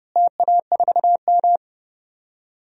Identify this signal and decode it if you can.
Morse code, heard as a keyed tone, about 29 wpm, 700 Hz. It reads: TA4M
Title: DXCC-100 Call Signs - 4 Characters - Repeated 3x - 30wpm